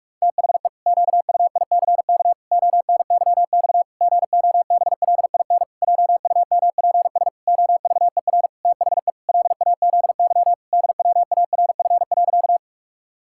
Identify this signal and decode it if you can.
Morse code, heard as a keyed tone, about 46 wpm, 700 Hz. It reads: THE QUICK ONYX GOBLIN JUMPS OVER THE LAZY DWARF.
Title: Pangram Challenge v2 - 45wpm